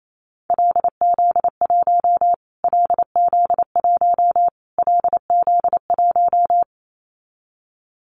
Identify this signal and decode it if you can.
Morse code, heard as a keyed tone, about 28 wpm, 700 Hz. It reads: LZ1 LZ1 LZ1
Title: Callsign Prefixes for Top 25 DXCC - Repeated 3x - 28wpm